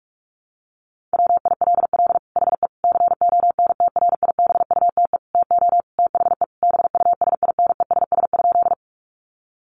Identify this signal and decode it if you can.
Morse code, heard as a keyed tone, about 44 wpm, 700 Hz. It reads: WILL HE CONTRIBUTE TO THE BUSINESS?
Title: Sentences from Top 1400 Words 45wpm